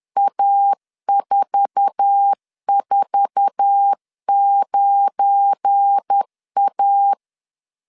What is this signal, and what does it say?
Morse code, heard as a keyed tone, about 10 wpm, 800 Hz. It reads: A449A
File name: LEZIONI / 8 YJ45 / 8 YJ4 10wpm 2633